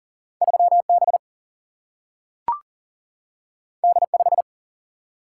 Morse code, heard as a keyed tone, about 40 wpm, 700 Hz. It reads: 3B D5
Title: Single-Letter Single-Number 40wpm